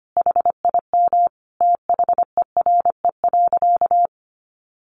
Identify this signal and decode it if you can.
Morse code, heard as a keyed tone, about 25 wpm, 700 Hz. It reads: HIM THERE.
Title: Sentences from Top 1100 Words 25wpm